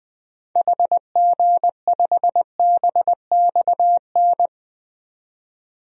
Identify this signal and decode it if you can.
Morse code, heard as a keyed tone, about 20 wpm, 700 Hz. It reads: HG5BXN
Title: DXCC-100 Call Signs - 6 Characters - 20wpm